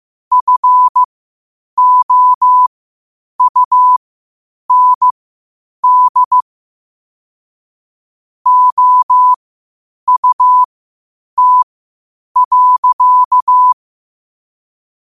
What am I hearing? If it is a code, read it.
Morse code, heard as a keyed tone, about 15 wpm, spaced as for 8 wpm, 1000 Hz. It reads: FOUND OUT.
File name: qotdfiles / 8wpm